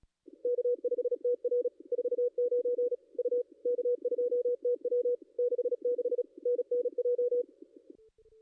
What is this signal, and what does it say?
Morse code, heard as a keyed tone, about 37 wpm, 440 Hz. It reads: K5TR 49 U K2TW 66 NNJ
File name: contests / ss-cw-k5tr-op-n5rz -2200